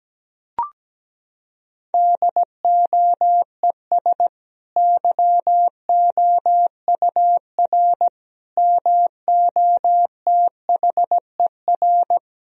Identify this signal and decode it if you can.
Morse code, heard as a keyed tone, about 17 wpm, 700 Hz. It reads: DOES YOUR MOTHER
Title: Sentences from Top 1000 Words - Review Part 4 - 17wpm